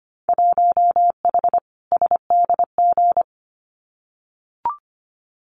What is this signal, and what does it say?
Morse code, heard as a keyed tone, about 25 wpm, 700 Hz. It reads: 1H SDG
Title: Winter Field Day Exchanges 25wpm